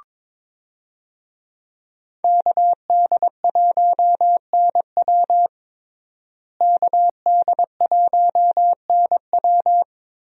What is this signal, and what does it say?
Morse code, heard as a keyed tone, about 22 wpm, 700 Hz. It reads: KD1NW KD1NW
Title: USA Call Signs - 5 Characters - Repeated 3x - 22wpm